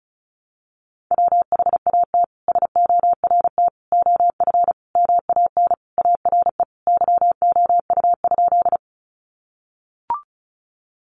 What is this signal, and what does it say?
Morse code, heard as a keyed tone, about 34 wpm, 700 Hz. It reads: WHAT SORT OF MAN ARE YOU?
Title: Sentences from Top 1300 Words 35wpm